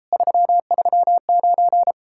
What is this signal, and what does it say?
Morse code, heard as a keyed tone, about 38 wpm, 700 Hz. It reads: 339